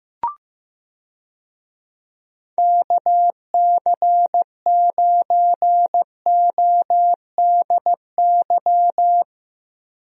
Morse code, heard as a keyed tone, about 15 wpm, 700 Hz. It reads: KC9ODY
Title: USA Call Signs - 6 Characters - Repeated 3x - 15wpm